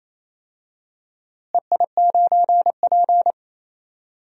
Morse code, heard as a keyed tone, about 27 wpm, 700 Hz. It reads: EI9P
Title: DXCC-100 Call Signs - 4 Characters - Repeated 3x - 28wpm